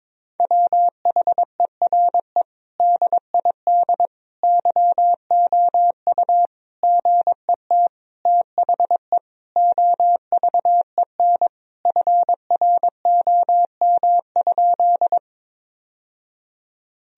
Morse code, heard as a keyed tone, about 22 wpm, 700 Hz. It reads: WHERE DID YOU GET THE OVEN FROM?
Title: Sentences from Top 1000 Words - Review Part 4 - 22wpm